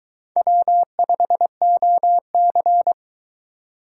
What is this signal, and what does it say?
Morse code, heard as a keyed tone, about 22 wpm, 700 Hz. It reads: W5OC